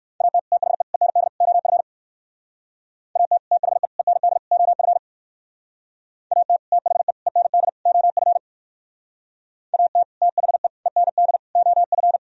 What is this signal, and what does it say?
Morse code, heard as a keyed tone, about 63 wpm, 700 Hz. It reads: AT THE END OF AT THE END OF AT THE END OF AT THE END OF